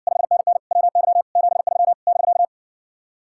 Morse code, heard as a keyed tone, about 59 wpm, 700 Hz. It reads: 5NN WY 73 <BK>